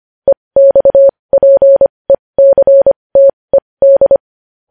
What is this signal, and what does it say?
Morse code, heard as a keyed tone, about 25 wpm, spaced as for 20 wpm, 550 Hz. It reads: EXPECTED
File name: podcast / News headlines-1x-25x20